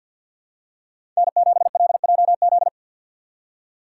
Morse code, heard as a keyed tone, about 47 wpm, 700 Hz. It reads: N7LJZ
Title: USA Call Signs - 3x Character Spacing - 50wpm